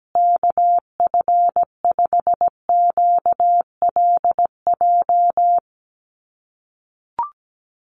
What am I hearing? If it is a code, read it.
Morse code, heard as a keyed tone, about 17 wpm, 700 Hz. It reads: KF5QLJ
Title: USA Call Signs Repeated 3x 17wpm